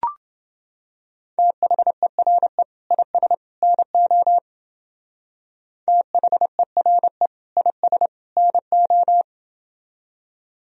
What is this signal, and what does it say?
Morse code, heard as a keyed tone, about 28 wpm, 700 Hz. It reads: THERE IS NO THERE IS NO